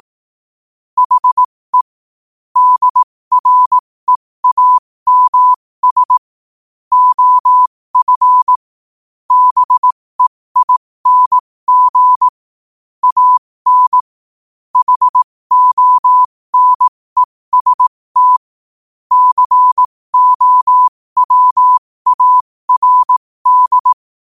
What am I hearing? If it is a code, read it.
Morse code, heard as a keyed tone, about 18 wpm, spaced as for 15 wpm, 1000 Hz. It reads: HE DREAMS OF BEING AN HONEST COWARD
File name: qotdfiles / 15wpm